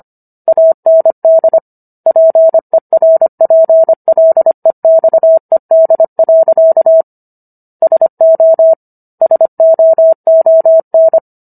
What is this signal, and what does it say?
Morse code, heard as a keyed tone, about 25 wpm, 650 Hz. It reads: AND PERPLEXED. SO SOON